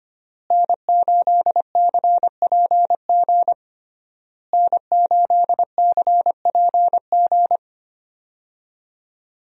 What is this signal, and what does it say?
Morse code, heard as a keyed tone, about 25 wpm, 700 Hz. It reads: N8CPG N8CPG
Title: USA Call Signs - 5 Characters - Repeated 3x - 25wpm